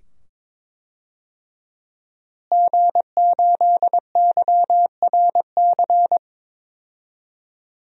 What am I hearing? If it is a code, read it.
Morse code, heard as a keyed tone, about 22 wpm, 700 Hz. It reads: G8YRC
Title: DXCC-100 Call Signs - 5 Characters - Repeated 2x - 22wpm